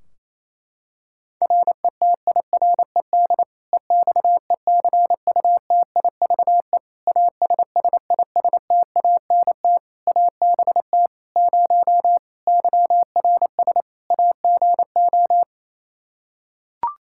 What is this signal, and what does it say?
Morse code, heard as a keyed tone, about 27 wpm, 700 Hz. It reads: RETIRED EXECUTIVE ASSISTANT ABT 0 YRS AGO
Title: K1USN SST QSO Practice 28wpm